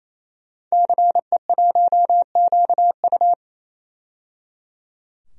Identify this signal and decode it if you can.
Morse code, heard as a keyed tone, about 28 wpm, 700 Hz. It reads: CE1QU
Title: DXCC-100 Call Signs - 5 Characters - 28wpm